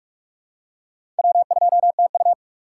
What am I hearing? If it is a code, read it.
Morse code, heard as a keyed tone, about 45 wpm, 700 Hz. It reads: W2TU
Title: USA Call Signs - 4 Characters - 4x Character Spacing - 45wpm